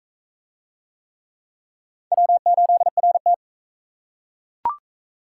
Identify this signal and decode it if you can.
Morse code, heard as a keyed tone, about 40 wpm, 700 Hz. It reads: W8RT